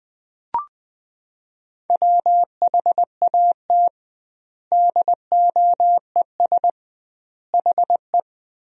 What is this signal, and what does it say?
Morse code, heard as a keyed tone, about 20 wpm, 700 Hz. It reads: WHAT DOES HE
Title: 2x Word-Spacing - Sentences from Top 200 Words - 20wpm